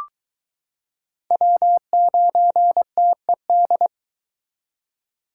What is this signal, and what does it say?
Morse code, heard as a keyed tone, about 22 wpm, 700 Hz. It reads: W9TED